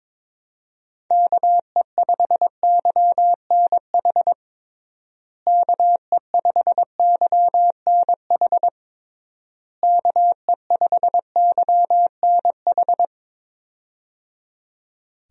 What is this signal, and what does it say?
Morse code, heard as a keyed tone, about 22 wpm, 700 Hz. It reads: KE5YNH KE5YNH KE5YNH
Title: USA Call Signs - 6 Characters - Repeated 3x - 22wpm